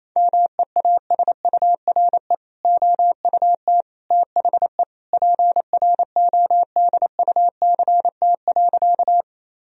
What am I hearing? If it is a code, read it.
Morse code, heard as a keyed tone, about 28 wpm, 700 Hz. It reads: MEASURE OUT THE PRODUCT.
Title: Sentences from Top 1000 Words - Review Part 2 - 28wpm